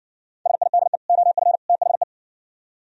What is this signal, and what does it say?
Morse code, heard as a keyed tone, about 59 wpm, 700 Hz. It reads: SIDE OF THE